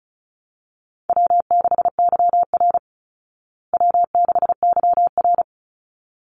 Morse code, heard as a keyed tone, about 34 wpm, 700 Hz. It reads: W6YR W6YR
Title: Easy Call Signs Repeated 2x - 35wpm